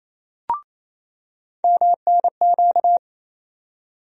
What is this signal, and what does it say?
Morse code, heard as a keyed tone, about 28 wpm, 700 Hz. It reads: MNQ
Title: USA Call Sign Postfixes - 3 Letters 28wpm